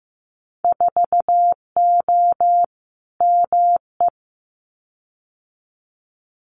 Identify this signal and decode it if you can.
Morse code, heard as a keyed tone, about 15 wpm, 700 Hz. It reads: 4O ME
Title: Winter Field Day Exchanges Encore 15wpm